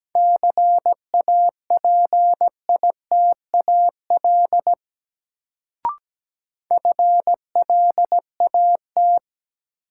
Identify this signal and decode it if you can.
Morse code, heard as a keyed tone, about 17 wpm, 700 Hz. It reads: CAPITAL FLAT
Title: Sentences from Top 1000 Words - New Word Drill - 17wpm